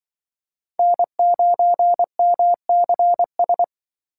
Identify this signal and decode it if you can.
Morse code, heard as a keyed tone, about 23 wpm, 700 Hz. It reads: N9MCS
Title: Speed Racing - USA 1x3 Call Signs 20wpm